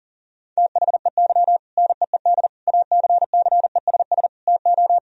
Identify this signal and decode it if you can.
Morse code, heard as a keyed tone, about 40 wpm, 700 Hz. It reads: THEY NEED ACCESS TO
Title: Sentences from Top 1000 Words - Review Part 4 - 40wpm